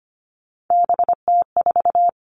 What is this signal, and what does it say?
Morse code, heard as a keyed tone, about 25 wpm, 700 Hz. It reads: BT4
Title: Three Challenging Letter-Number Combinations 25wpm